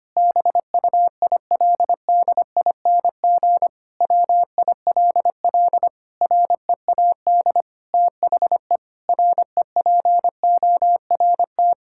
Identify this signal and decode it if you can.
Morse code, heard as a keyed tone, about 25 wpm, 700 Hz. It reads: BUILDING WILL READ THE REPORT
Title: Sentences from Top 1000 Words - Review Part 6 - 25wpm